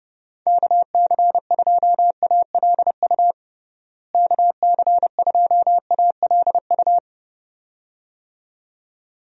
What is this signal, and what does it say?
Morse code, heard as a keyed tone, about 32 wpm, 700 Hz. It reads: KC2ALU KC2ALU
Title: USA Call Signs - 6 Characters - Repeated 2x - 30wpm